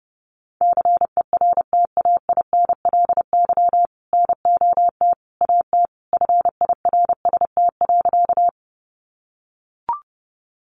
Morse code, heard as a keyed tone, about 30 wpm, 700 Hz. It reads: CERTAINLY NOT AT FIRST.
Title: Sentences from Top 900 Words 30wpm